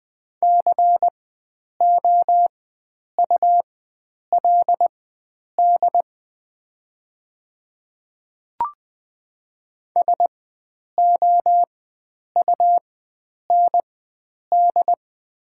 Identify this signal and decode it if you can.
Morse code, heard as a keyed tone, about 20 wpm, spaced as for 9 wpm, 700 Hz. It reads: COULD SOUND
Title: 4x Character Spacing - Top 100 Words - 20wpm